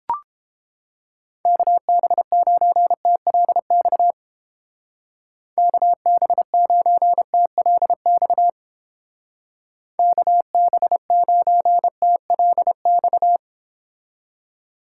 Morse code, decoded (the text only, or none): KB9TLX KB9TLX KB9TLX